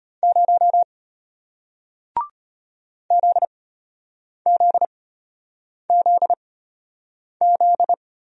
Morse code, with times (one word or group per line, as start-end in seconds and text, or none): 0.2-0.8 0
3.1-7.9 ZZZZ